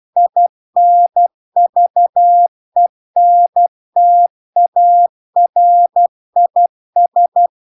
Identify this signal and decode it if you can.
Morse code, heard as a keyed tone, about 12 wpm, 700 Hz. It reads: INVENTARIS
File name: cw-cursus-VERON-a03